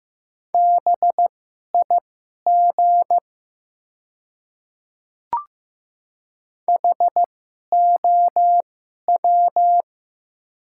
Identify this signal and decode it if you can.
Morse code, heard as a keyed tone, about 15 wpm, spaced as for 11 wpm, 700 Hz. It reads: BIG HOW
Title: 2x Character Spacing - Top 200 Words - 15wpm